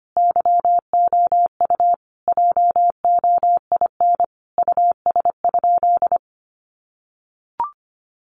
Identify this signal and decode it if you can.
Morse code, heard as a keyed tone, about 25 wpm, 700 Hz. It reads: YOU JOIN US?
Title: Sentences from Top 1400 Words 25wpm